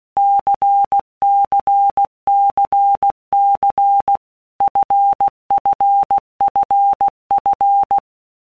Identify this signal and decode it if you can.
Morse code, heard as a keyed tone, about 16 wpm, 800 Hz. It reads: CCCC FFFF
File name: Basic Letter Groups By Blocks / Groups@15 / LFCYQ@15